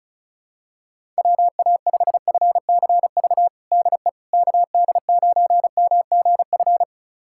Morse code, heard as a keyed tone, about 36 wpm, 700 Hz. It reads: WA5FCV DE KD9MGF
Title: USA Callsign DE Callsign - 6 Characters - 35wpm